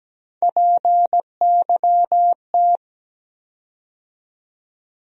Morse code, heard as a keyed tone, about 17 wpm, 700 Hz. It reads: PYT
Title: USA Call Sign Postfixes - 3 Letters 17wpm